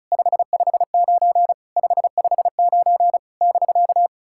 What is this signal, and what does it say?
Morse code, heard as a keyed tone, about 36 wpm, 690 Hz. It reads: 559 559 <BK>
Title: SOTA QSO Practice 35wpm